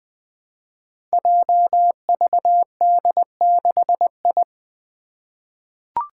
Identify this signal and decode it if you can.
Morse code, heard as a keyed tone, about 20 wpm, 700 Hz. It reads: JVD6I
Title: Five Challenging Letter-Number Combinations 20wpm